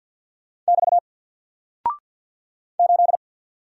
Morse code, heard as a keyed tone, about 49 wpm, 700 Hz. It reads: <BT> 8
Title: Single Character (Letter:Number:Punctuation:Prosign) Alt 50wpm